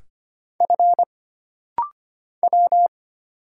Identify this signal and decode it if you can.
Morse code, heard as a keyed tone, about 25 wpm, 700 Hz. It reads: F W
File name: Single Character (Letter:Number:Punctuation:Prosign) - Rapid-Fire - 25wpm